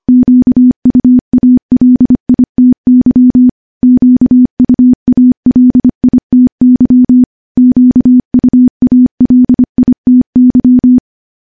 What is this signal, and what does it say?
Morse code, heard as a keyed tone, about 25 wpm, 250 Hz. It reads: QUALITY QUALITY QUALITY
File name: morse / morse-training-25WPM